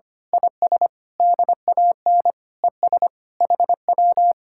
Morse code, heard as a keyed tone, about 25 wpm, 700 Hz. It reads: IS DAN ES HW
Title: Speed-Racing Common QSO Phrases - 25wpm